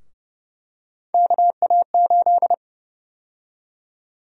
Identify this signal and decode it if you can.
Morse code, heard as a keyed tone, about 30 wpm, 700 Hz. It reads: KA8
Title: USA Call Sign Prefixes - Repeated 2x - 30wpm